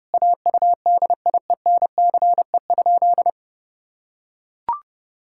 Morse code, heard as a keyed tone, about 32 wpm, 700 Hz. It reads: AUDIENCE?
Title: Sentences from Top 1000 Words - Review Part 5 - 30wpm